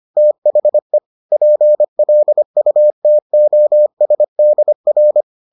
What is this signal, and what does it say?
Morse code, heard as a keyed tone, about 25 wpm, 600 Hz. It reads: THE PLUTOSDR